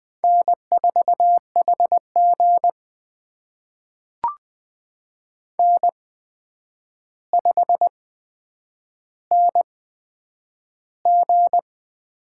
Morse code, heard as a keyed tone, about 20 wpm, spaced as for 9 wpm, 700 Hz. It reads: N4HG N5NG
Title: USA Call Signs - 4 Characters - 8x Character Spacing - 20wpm